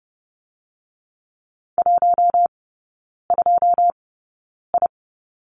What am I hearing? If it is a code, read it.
Morse code, heard as a keyed tone, about 30 wpm, spaced as for 9 wpm, 700 Hz. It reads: 12I
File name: course / 30:10wpm / 087 Lesson 15e - Number 5 - Three Characters 30wpm